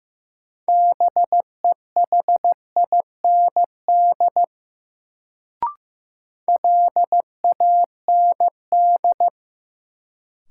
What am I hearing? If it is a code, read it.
Morse code, heard as a keyed tone, about 15 wpm, 700 Hz. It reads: BEHIND LAND
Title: Sentences from Top 500 Words - New Word Drill - 15wpm